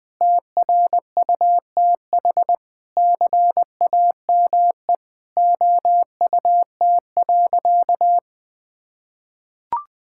Morse code, heard as a keyed tone, about 21 wpm, 700 Hz. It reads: TRUTH CAME OUT.
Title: Sentences with 9-Word Phrases - Spoken Beginning - 20wpm